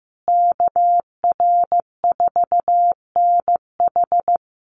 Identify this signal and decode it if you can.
Morse code, heard as a keyed tone, about 15 wpm, 700 Hz. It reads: KR4NH
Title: USA Call Signs Repeated 3x 15wpm